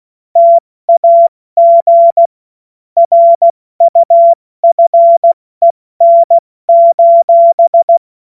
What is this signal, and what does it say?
Morse code, heard as a keyed tone, about 16 wpm, 680 Hz. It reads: TAG RUFEN:
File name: Morsekurs / Witze / Witz 08. August 2023 70 F80